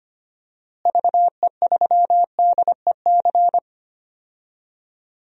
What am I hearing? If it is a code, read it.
Morse code, heard as a keyed tone, about 25 wpm, 700 Hz. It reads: VE3DEC
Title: DXCC-100 Call Signs - 6 Characters - Repeated 3x - 25wpm